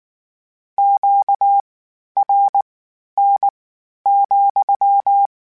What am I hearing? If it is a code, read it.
Morse code, heard as a keyed tone, about 19 wpm, spaced as for 11 wpm, 800 Hz. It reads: QRN,